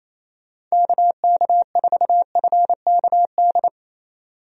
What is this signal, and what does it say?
Morse code, heard as a keyed tone, about 27 wpm, 700 Hz. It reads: KK4FKD
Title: USA Call Signs - 6 Characters - 28wpm